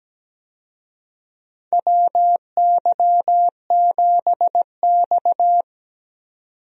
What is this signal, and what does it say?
Morse code, heard as a keyed tone, about 17 wpm, 700 Hz. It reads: WY7X
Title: USA Call Signs - 4 Characters - 17wpm